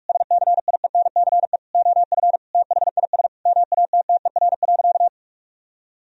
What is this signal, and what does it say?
Morse code, heard as a keyed tone, about 41 wpm, 700 Hz. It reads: SCIENCE OF THIS MATTER.
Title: Sentences from Top 1000 Words - Review Part 2 - 45wpm